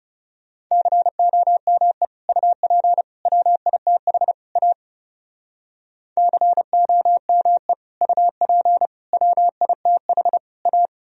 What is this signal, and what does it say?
Morse code, heard as a keyed tone, about 38 wpm, 700 Hz. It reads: COME UP WITH A COME UP WITH A